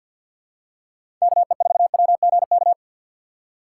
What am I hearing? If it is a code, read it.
Morse code, heard as a keyed tone, about 52 wpm, 700 Hz. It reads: KE4WGK
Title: USA Call Signs - 6 Characters - 3x Character Spacing - 50wpm